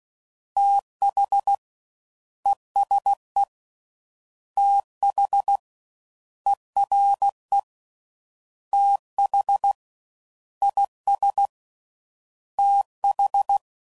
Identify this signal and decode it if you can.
Morse code, heard as a keyed tone, about 16 wpm, 800 Hz. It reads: TH ESE TH ERE TH IS TH